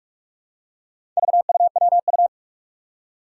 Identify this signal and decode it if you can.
Morse code, heard as a keyed tone, about 45 wpm, 700 Hz. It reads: VUWU
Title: Four Challenging Letter-Number Combinations 45wpm